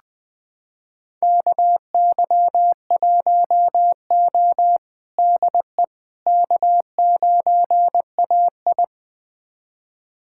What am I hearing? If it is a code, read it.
Morse code, heard as a keyed tone, about 20 wpm, 700 Hz. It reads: KY1O DE K9AI
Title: USA Callsign DE Callsign - 4 Characters - 20wpm